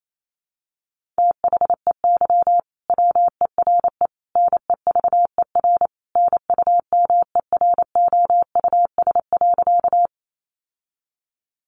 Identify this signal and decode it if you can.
Morse code, heard as a keyed tone, about 28 wpm, 700 Hz. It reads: THEY WERE NEVER NUMEROUS.
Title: Sentences from Top 1400 Words 28wpm